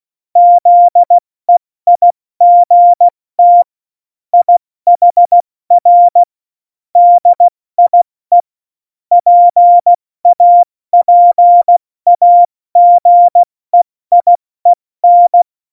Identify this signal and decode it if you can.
Morse code, heard as a keyed tone, about 16 wpm, 700 Hz. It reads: ZEIGT IHR DIE PAPAGEIEN
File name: Morsekurs / Witze / Witz 20. November 2018, 70 F80